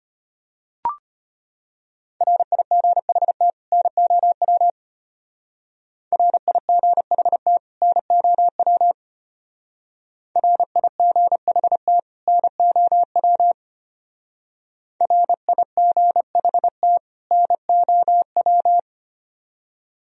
Morse code, heard as a keyed tone, about 30 wpm, 700 Hz. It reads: RIGHT NOW RIGHT NOW RIGHT NOW RIGHT NOW
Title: Speed-Racing - Sets of 2 Words 25wpm